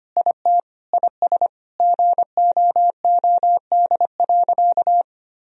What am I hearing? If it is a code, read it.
Morse code, heard as a keyed tone, about 25 wpm, 700 Hz. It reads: IT IS GOOD.
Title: Sentences from Top 1000 Words - Review Part 3 - 25wpm